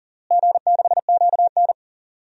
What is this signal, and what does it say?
Morse code, heard as a keyed tone, about 37 wpm, 700 Hz. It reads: G6QN
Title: DXCC-100 Call Signs - 4 Characters - Repeated 3x - 40wpm